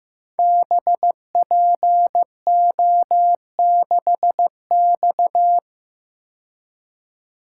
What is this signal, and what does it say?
Morse code, heard as a keyed tone, about 15 wpm, 700 Hz. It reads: BPO6X
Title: Code Groups - 5 Letter:Numbers - 15wpm